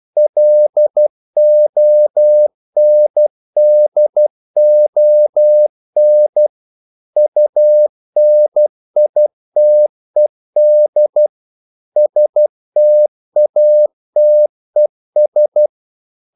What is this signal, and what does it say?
Morse code, heard as a keyed tone, about 12 wpm, 600 Hz. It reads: LONDON UNITED STATES
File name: CW / Practice / NationalCapitals12